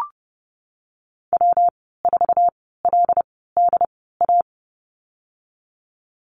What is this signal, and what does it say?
Morse code, heard as a keyed tone, about 30 wpm, spaced as for 17 wpm, 700 Hz. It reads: W4LDA
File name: Lesson 19g - Letter w - Call Signs 30wpm